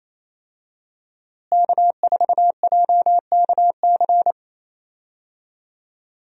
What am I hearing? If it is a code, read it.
Morse code, heard as a keyed tone, about 28 wpm, 700 Hz. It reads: K4JKC
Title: USA Call Signs - 5 Characters - Repeated 3x - 28wpm